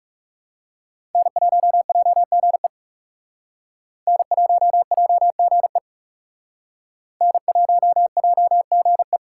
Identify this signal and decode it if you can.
Morse code, heard as a keyed tone, about 37 wpm, 700 Hz. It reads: N1JGE N1JGE N1JGE
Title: Speed Racing - USA 1x3 Call Signs 30wpm